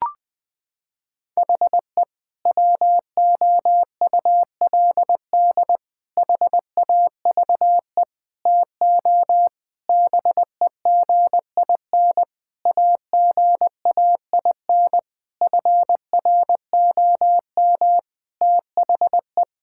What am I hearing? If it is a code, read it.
Morse code, heard as a keyed tone, about 21 wpm, 700 Hz. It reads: HE WOULD HAVE TO BEGIN AGAIN FROM THE
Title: Sentences from Top 1000 Words - Review Part 5 - 20wpm